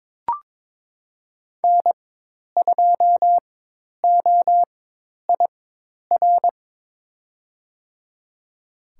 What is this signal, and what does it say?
Morse code, heard as a keyed tone, about 22 wpm, spaced as for 10 wpm, 700 Hz. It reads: N2OIR
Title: USA Call Signs - 5 Characters - 4x Character Spacing - 22wpm